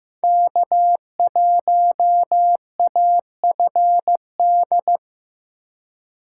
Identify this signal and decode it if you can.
Morse code, heard as a keyed tone, about 15 wpm, 700 Hz. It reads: K1AFD
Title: USA Call Signs - 5 Characters - 3x Character Spacing - 15wpm